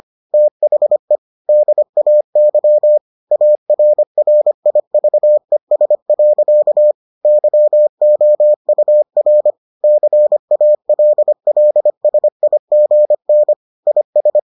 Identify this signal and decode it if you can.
Morse code, heard as a keyed tone, about 25 wpm, 600 Hz. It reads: THE DAY ARRIVES. YOUR CALLSIGN IS